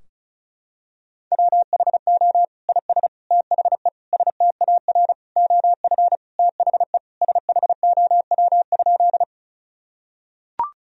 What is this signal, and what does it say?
Morse code, heard as a keyed tone, about 34 wpm, 700 Hz. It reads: WHO IS THE STAR OF THE SHOW?
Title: Sentences from Top 1000 Words - Review Part 2 - 35wpm